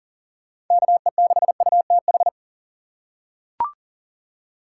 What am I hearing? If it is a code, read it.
Morse code, heard as a keyed tone, about 40 wpm, 700 Hz. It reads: KE6UTH
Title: USA Call Signs - 6 Characters - 4x Character Spacing - 40wpm